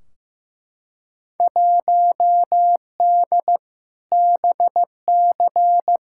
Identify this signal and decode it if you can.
Morse code, heard as a keyed tone, about 15 wpm, 700 Hz. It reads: 1D BC
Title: Field Day Exchanges - 15wpm